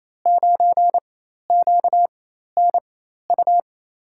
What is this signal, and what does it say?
Morse code, heard as a keyed tone, about 28 wpm, spaced as for 13 wpm, 700 Hz. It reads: 9QNU
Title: USA Call Signs - 4x Character Spacing - 28wpm